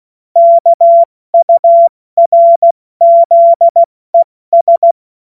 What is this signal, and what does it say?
Morse code, heard as a keyed tone, about 16 wpm, spaced as for 14 wpm, 680 Hz. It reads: KURZES
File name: Morsekurs / Witze / Witz 09. November 2021, 70 F80